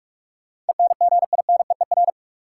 Morse code, heard as a keyed tone, about 46 wpm, 700 Hz. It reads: ENGINEER
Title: K1USN SST QSO Practice 45wpm